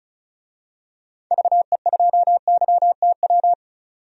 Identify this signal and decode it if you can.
Morse code, heard as a keyed tone, about 34 wpm, 700 Hz. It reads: VE2YTW
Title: DXCC-100 Call Signs - 6 Characters - 35wpm